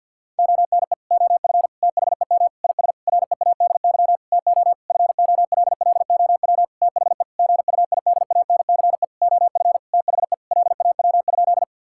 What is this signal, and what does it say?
Morse code, heard as a keyed tone, about 50 wpm, 700 Hz. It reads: ONE OF THEM IS READY TO FOLLOW THE GUIDANCE OF THE LAW?
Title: Sentences from Top 1600 Words 50wpm